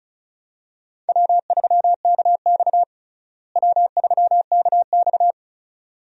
Morse code, heard as a keyed tone, about 34 wpm, 700 Hz. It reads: W3KX W3KX
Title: USA Call Signs - 4 Characters - Repeated 2x - 35wpm